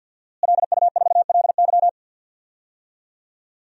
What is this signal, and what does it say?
Morse code, heard as a keyed tone, about 52 wpm, 700 Hz. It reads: LU4LY
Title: DXCC-100 Call Signs - 5 Characters - Repeated 3x - 50wpm